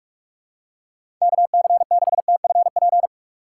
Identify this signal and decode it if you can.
Morse code, heard as a keyed tone, about 44 wpm, 700 Hz. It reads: KC6TFP
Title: USA Call Signs - 6 Characters - 6x Character Spacing - 45wpm